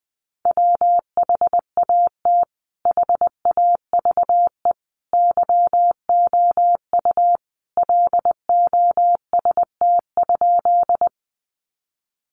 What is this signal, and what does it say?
Morse code, heard as a keyed tone, about 20 wpm, 700 Hz. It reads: WHAT HAVE YOU LOST?
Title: Sentences from Top 1400 Words 20wpm